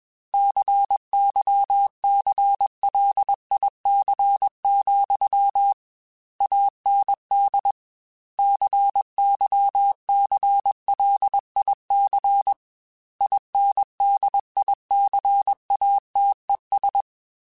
Morse code, heard as a keyed tone, about 21 wpm, 800 Hz. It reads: CYCLIC, AND CYCLIC INDICATES
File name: Technical Subjects / Technical Subjects-mp3s / XC @20